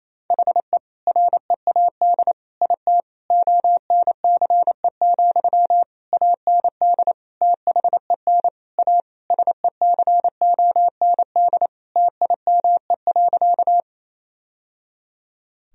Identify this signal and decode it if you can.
Morse code, heard as a keyed tone, about 28 wpm, 700 Hz. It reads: HE READ IT ONCE, AND THEN A SECOND TIME.
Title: Sentences from Top 1000 Words - Review Part 1 - 28wpm